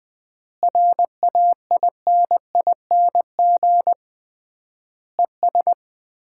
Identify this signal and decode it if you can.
Morse code, heard as a keyed tone, about 21 wpm, 700 Hz. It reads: RAINING ES
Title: Common QSO Phrases - 3x Word-Spacing - 20wpm